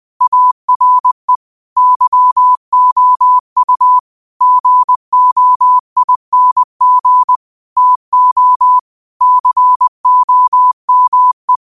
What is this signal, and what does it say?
Morse code, heard as a keyed tone, about 20 wpm, 1000 Hz. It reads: ARE YOU GOING TO COME
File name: qotdfiles / 20wpm